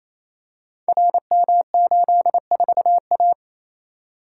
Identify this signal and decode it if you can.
Morse code, heard as a keyed tone, about 29 wpm, 690 Hz. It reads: RM84A
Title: Code Groups - 5 Letter:Numbers - 28wpm